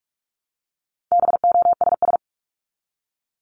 Morse code, heard as a keyed tone, about 46 wpm, 700 Hz. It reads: BOSS
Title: Top 501-750 Words 45wpm